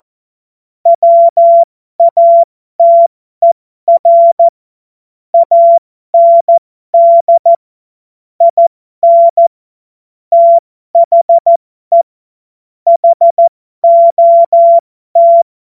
Morse code, heard as a keyed tone, about 14 wpm, spaced as for 12 wpm, 680 Hz. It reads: WATER AND IN THE HOT
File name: Morsekurs / Witze